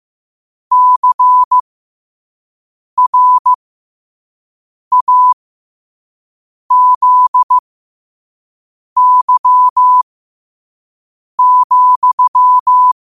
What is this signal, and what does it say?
Morse code, heard as a keyed tone, about 15 wpm, spaced as for 5 wpm, 1000 Hz. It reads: CRAZY,
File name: qotdfiles / 5wpm